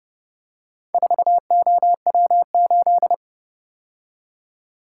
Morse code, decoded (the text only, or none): 4OW8